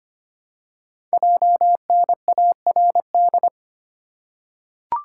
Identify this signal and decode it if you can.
Morse code, heard as a keyed tone, about 25 wpm, 700 Hz. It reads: JNARD
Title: Code Groups - 5 Letters - 25wpm